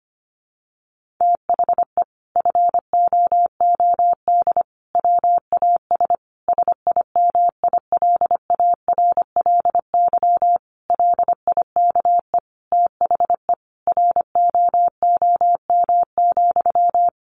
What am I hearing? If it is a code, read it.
Morse code, heard as a keyed tone, about 25 wpm, 700 Hz. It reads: THE FOOD WAS SIMILARLY LIKE THE ROOM,
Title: Sentences from Top 1400 Words 25wpm